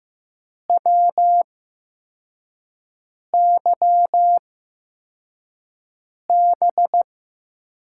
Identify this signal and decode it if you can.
Morse code, heard as a keyed tone, about 15 wpm, spaced as for 4 wpm, 700 Hz. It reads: WYB
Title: USA Call Signs - 5 Characters - 8x Character Spacing - 15wpm